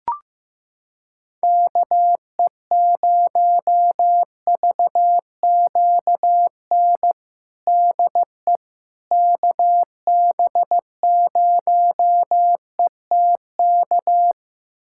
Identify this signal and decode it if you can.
Morse code, heard as a keyed tone, about 15 wpm, 700 Hz. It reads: KE0VQN DE KB0ETK
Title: USA Callsign DE Callsign 15wpm